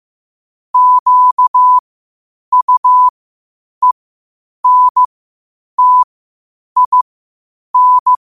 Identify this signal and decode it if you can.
Morse code, heard as a keyed tone, about 15 wpm, spaced as for 8 wpm, 1000 Hz. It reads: QUENTIN
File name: qotdfiles / 8wpm